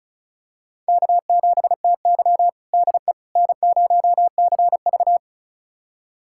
Morse code, heard as a keyed tone, about 34 wpm, 700 Hz. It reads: K7TY DE N0CV
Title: USA Callsign DE Callsign - 4 Characters - 35wpm